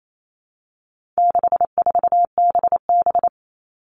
Morse code, heard as a keyed tone, about 28 wpm, 700 Hz. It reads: 64BB
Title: All Cut Numbers - 4-Digit 28wpm